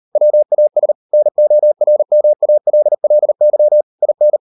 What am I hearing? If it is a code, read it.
Morse code, heard as a keyed tone, about 38 wpm, 600 Hz. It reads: WAS NORMALLY IN